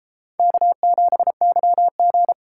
Morse code, heard as a keyed tone, about 33 wpm, 700 Hz. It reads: K7YG